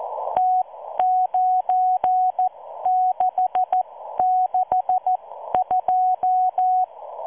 Morse code, heard as a keyed tone, about 14 wpm, spaced as for 12 wpm, 740 Hz. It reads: T9662